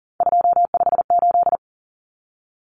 Morse code, heard as a keyed tone, about 40 wpm, 700 Hz. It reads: 258
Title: Numbers - 3-Digit 40wpm